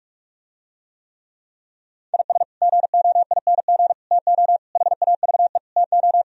Morse code, heard as a keyed tone, about 47 wpm, 700 Hz. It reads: IS GOING TO HAVE TO